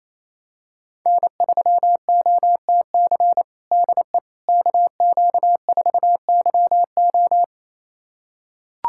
Morse code, heard as a keyed tone, about 28 wpm, 700 Hz. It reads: N3OTC DE KQ4YO
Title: USA Callsign DE Callsign - 5 Characters - 28wpm